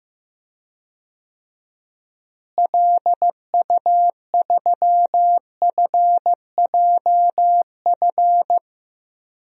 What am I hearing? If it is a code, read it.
Morse code, heard as a keyed tone, about 15 wpm, 700 Hz. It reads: LU3FJF
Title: DXCC-100 Call Signs - 6 Characters - Repeated 2x - 15wpm